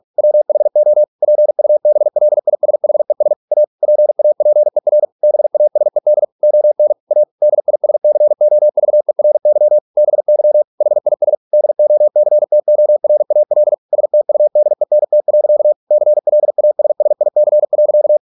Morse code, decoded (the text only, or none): WHO PUBLISHES A PAPER BASED ON A DISCOVERY BY HIS DOCTORAL STUDENT. CLASSIC.